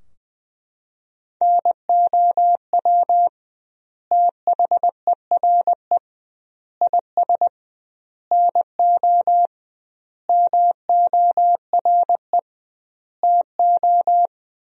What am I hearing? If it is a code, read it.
Morse code, heard as a keyed tone, about 20 wpm, 700 Hz. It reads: NOW THERE IS NO MORE TO
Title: 2x Word-Spacing - Sentences from Top 100 Words - 20wpm